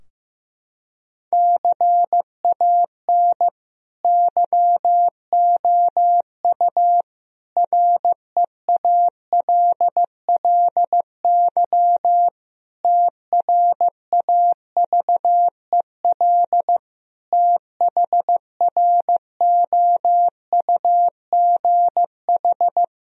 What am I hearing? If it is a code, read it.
Morse code, heard as a keyed tone, about 15 wpm, 700 Hz. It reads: CAN YOU REALLY TRAVEL THROUGH
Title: Sentences from Top 1000 Words - Review Part 2 - 15wpm